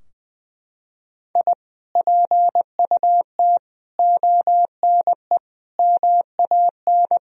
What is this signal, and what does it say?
Morse code, heard as a keyed tone, about 20 wpm, 700 Hz. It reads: I PUT ONE MAN
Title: Sentences from Top 1000 Words - Review Part 2 - 20wpm